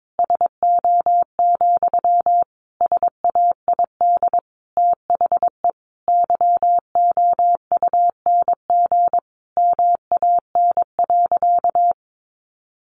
Morse code, heard as a keyed tone, about 22 wpm, 700 Hz. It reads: SO, SAID THE YOUNG MAN.
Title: Sentences from Top 900 Words 22wpm